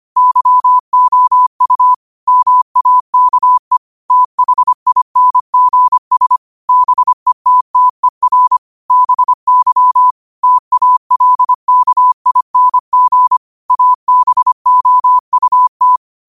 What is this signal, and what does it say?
Morse code, heard as a keyed tone, about 25 wpm, 1000 Hz. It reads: YOU MAKE THINGS BETTER BY TALKING ABOUT
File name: qotdfiles / 25wpm